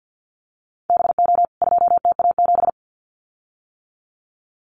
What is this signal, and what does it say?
Morse code, heard as a keyed tone, about 50 wpm, 700 Hz. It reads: BO 2TA7
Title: CWT Exchange 50wpm